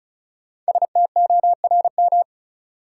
Morse code, heard as a keyed tone, about 36 wpm, 700 Hz. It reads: STORM
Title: Five Letter Words - 35wpm